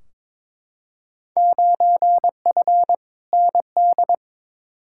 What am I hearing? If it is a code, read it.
Morse code, heard as a keyed tone, about 22 wpm, 700 Hz. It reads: 9F ND
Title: Field Day Exchanges Encore - 22wpm